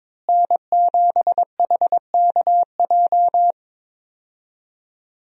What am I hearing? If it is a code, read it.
Morse code, heard as a keyed tone, about 22 wpm, 700 Hz. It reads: N7HKJ
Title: USA Call Signs - 5 Characters - Repeated 3x - 22wpm